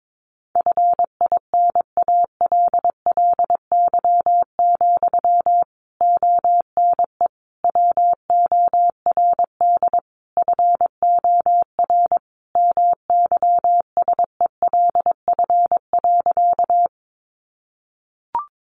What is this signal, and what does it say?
Morse code, heard as a keyed tone, about 22 wpm, 700 Hz. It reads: FINALLY, ONE WORD FOR MYSELF.
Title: Sentences from Top 1300 Words 22wpm